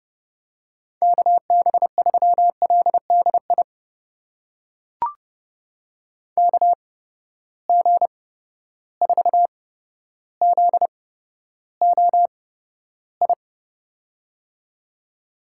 Morse code, heard as a keyed tone, about 29 wpm, spaced as for 13 wpm, 700 Hz. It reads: KB3LDI KG4ZOI
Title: USA Call Signs - 6 Characters - 8x Character Spacing - 30wpm